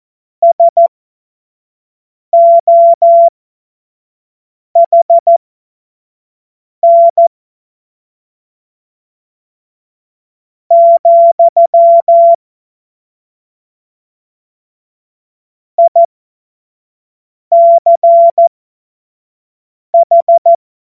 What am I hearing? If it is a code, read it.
Morse code, heard as a keyed tone, about 14 wpm, spaced as for 5 wpm, 680 Hz. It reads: SOHN , ICH
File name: Morsekurs / Witze / Witz 28. Dezember 2021, 25 F70